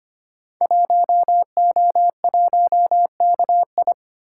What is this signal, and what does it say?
Morse code, heard as a keyed tone, about 25 wpm, 700 Hz. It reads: 1O1KI
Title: Code Groups - 5 Letter:Numbers - 25wpm